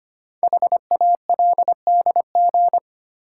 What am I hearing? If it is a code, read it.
Morse code, heard as a keyed tone, about 25 wpm, 700 Hz. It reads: HALDG
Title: Code Groups - 5 Letters - 25wpm